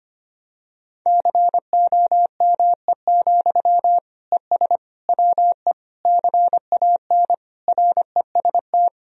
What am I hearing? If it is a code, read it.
Morse code, heard as a keyed tone, about 25 wpm, 700 Hz. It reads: COME, ES WE CAN REST
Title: Sentences from Top 500 Words and Abbreviations 25wpm